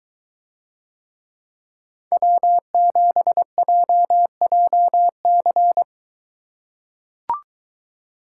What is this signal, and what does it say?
Morse code, heard as a keyed tone, about 23 wpm, 700 Hz. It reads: W7JJC